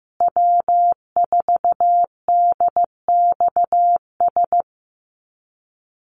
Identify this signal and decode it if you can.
Morse code, heard as a keyed tone, about 15 wpm, 700 Hz. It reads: W4DXS
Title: USA Call Signs Repeated 2x 15wpm